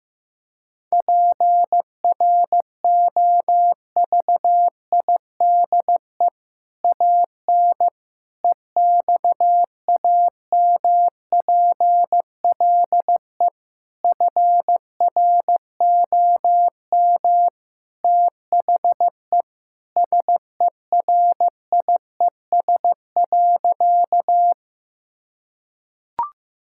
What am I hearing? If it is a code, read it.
Morse code, heard as a keyed tone, about 15 wpm, 700 Hz. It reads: PROVIDE AN EXAMPLE FROM THE SERIES.
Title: Sentences from Top 1000 Words - Review Part 5 - 15wpm